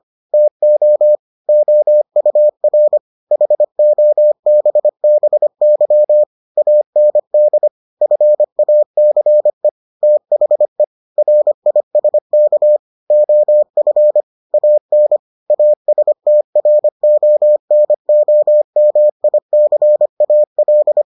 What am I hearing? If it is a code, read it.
Morse code, heard as a keyed tone, about 25 wpm, 600 Hz. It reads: TO OUR HOBBY AND FACE THE RISK OF AN ASTRONOMICAL